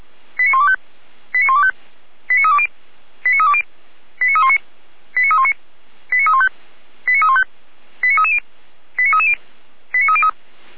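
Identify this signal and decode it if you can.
Five-tone selective call (ZVEI), heard as quick runs of five tones, tones of about 70 ms. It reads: 89125 89125 89320 89320 89319 89319 89315 89315 89330 89330 89303
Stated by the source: Received on an SDR, ZVEI five-tone selective call.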